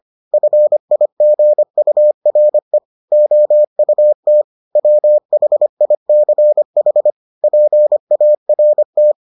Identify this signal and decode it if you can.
Morse code, heard as a keyed tone, about 25 wpm, 600 Hz. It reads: FIGURE OUT WHICH PART